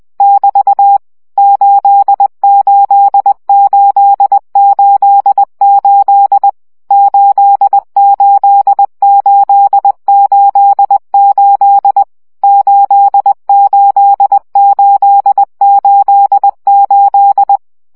Morse code, heard as a keyed tone, about 20 wpm, 800 Hz. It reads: <BT> 88888 88888 88888
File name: CW gyakorlas 8 17 baud